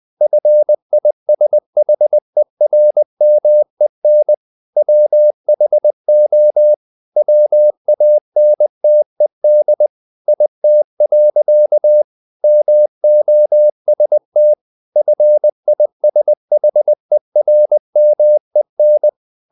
Morse code, heard as a keyed tone, about 20 wpm, 600 Hz. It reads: FISHERMEN WHO WANTED IT. MOST FISHERMEN
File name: the old man and the sea / the old man and the sea part 1 20wpm